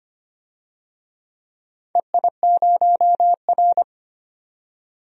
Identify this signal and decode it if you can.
Morse code, heard as a keyed tone, about 25 wpm, 700 Hz. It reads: EI0R
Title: DXCC-100 Call Signs - 4 Characters - 25wpm